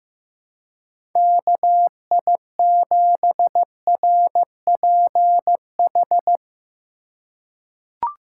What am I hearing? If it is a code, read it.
Morse code, heard as a keyed tone, about 15 wpm, 700 Hz. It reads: KI7RPH
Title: USA Call Signs - 3x Character Spacing - 15wpm